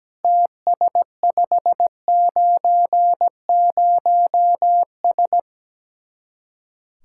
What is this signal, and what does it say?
Morse code, heard as a keyed tone, about 17 wpm, 700 Hz. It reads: TS590S
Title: Speed-Racing Common QSO Phrases - 17wpm